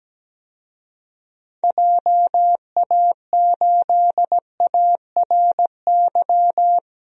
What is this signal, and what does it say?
Morse code, heard as a keyed tone, about 17 wpm, 700 Hz. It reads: JA8ARY
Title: DXCC-100 Call Signs - 6 Characters - Repeated 2x - 17wpm